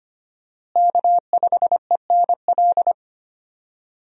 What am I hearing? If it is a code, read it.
Morse code, heard as a keyed tone, about 25 wpm, 700 Hz. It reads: K5ENL
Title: USA Call Signs - 5 Characters - Repeated 3x - 25wpm